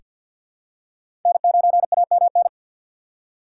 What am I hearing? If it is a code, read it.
Morse code, heard as a keyed tone, about 46 wpm, 700 Hz. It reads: N9AMN
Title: USA Call Signs - 5 Characters - 6x Character Spacing - 50wpm